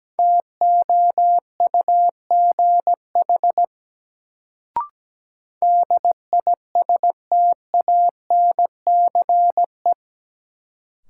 Morse code, heard as a keyed tone, about 17 wpm, 700 Hz. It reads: TOUGH DISTANCE
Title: Sentences from Top 1200 Words - New Word Drill Alt - 17wpm